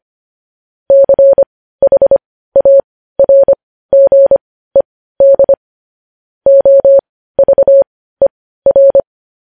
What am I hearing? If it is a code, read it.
Morse code, heard as a keyed tone, about 25 wpm, spaced as for 15 wpm, 550 Hz. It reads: CHARGED OVER
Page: news headlines in morse code at 15 wpm